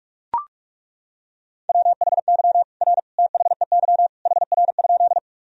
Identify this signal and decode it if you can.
Morse code, heard as a keyed tone, about 41 wpm, 700 Hz. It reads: WHY R THEY HR?
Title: Sentences from Top 200 Words and Abbreviations 45wpm